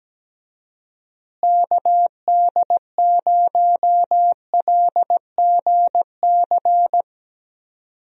Morse code, heard as a keyed tone, about 17 wpm, 700 Hz. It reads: KD0LGC
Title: USA Call Signs - 6 Characters - Repeated 2x - 17wpm